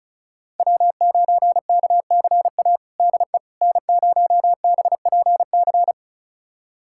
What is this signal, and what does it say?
Morse code, heard as a keyed tone, about 36 wpm, 690 Hz. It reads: W9KCA DE N0BPC
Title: USA Callsign DE Callsign - 5 Characters - 35wpm